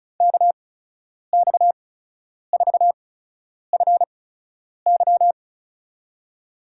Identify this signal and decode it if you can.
Morse code, heard as a keyed tone, about 36 wpm, spaced as for 10 wpm, 700 Hz. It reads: KX4FY
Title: USA Call Signs - 5 Characters - 8x Character Spacing - 35wpm